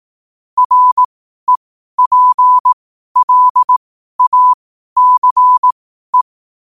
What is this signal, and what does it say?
Morse code, heard as a keyed tone, about 18 wpm, spaced as for 12 wpm, 1000 Hz. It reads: REPLACE
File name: qotdfiles / 12wpm